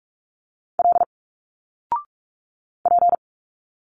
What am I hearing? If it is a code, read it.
Morse code, heard as a keyed tone, about 46 wpm, 700 Hz. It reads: L P
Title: Single Letters 45wpm